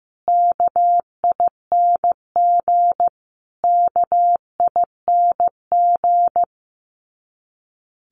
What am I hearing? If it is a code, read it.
Morse code, heard as a keyed tone, about 15 wpm, 700 Hz. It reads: KING KING
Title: Top 500 Words Repeated 15wpm